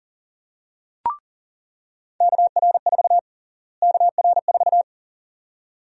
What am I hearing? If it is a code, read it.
Morse code, heard as a keyed tone, about 40 wpm, 700 Hz. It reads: KR4 KR4
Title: USA Call Sign Prefixes - Repeated 2x - 40wpm